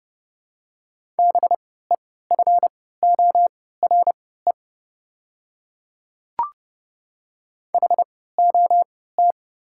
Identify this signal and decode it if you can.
Morse code, heard as a keyed tone, about 30 wpm, spaced as for 17 wpm, 700 Hz. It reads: BEFORE HOT
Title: 3x Character Spacing - Top 200 Words - 30wpm